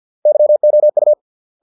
Morse code, heard as a keyed tone, about 50 wpm, 600 Hz. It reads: YOU